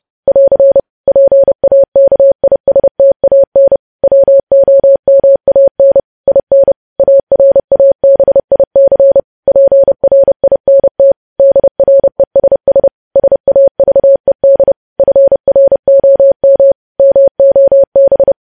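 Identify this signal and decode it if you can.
Morse code, heard as a keyed tone, about 30 wpm, 550 Hz. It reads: <AR> PAKISTAN WOMAN IN ARABIC PRINT DRESS SAVED FROM MOB
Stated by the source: News headlines sent in Morse at 30 words per minute.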